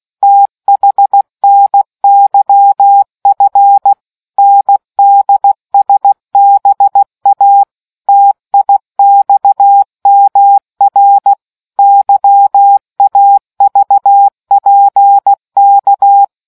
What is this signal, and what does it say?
Morse code, heard as a keyed tone, about 16 wpm, 800 Hz. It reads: THNYF NDSBA TIXMR YAVPK